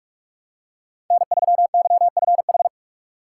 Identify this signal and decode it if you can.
Morse code, heard as a keyed tone, about 46 wpm, 700 Hz. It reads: N3YFH
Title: USA Call Signs - 6x Character Spacing - 45wpm